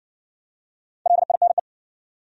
Morse code, heard as a keyed tone, about 55 wpm, 700 Hz. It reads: LINE